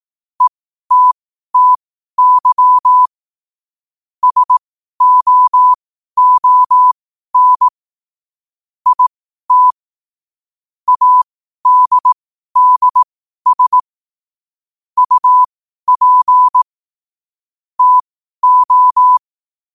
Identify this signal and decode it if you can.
Morse code, heard as a keyed tone, about 18 wpm, spaced as for 12 wpm, 1000 Hz. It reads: ETTY SOON IT ADDS UP TO
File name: qotdfiles / 12wpm